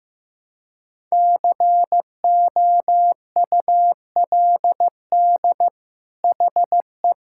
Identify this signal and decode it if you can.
Morse code, heard as a keyed tone, about 15 wpm, 700 Hz. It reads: COULD HE
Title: Sentences from Top 1000 Words - Review Part 6 - 15wpm